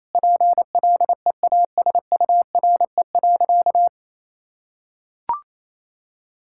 Morse code, heard as a keyed tone, about 28 wpm, 700 Hz. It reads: PLEASURE.
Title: Sentences from Top 1600 Words 28wpm